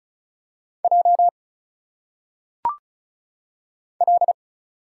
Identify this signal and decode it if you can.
Morse code, heard as a keyed tone, about 34 wpm, 700 Hz. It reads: J L
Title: Single Letters - Phonetic Pronunciation 35wpm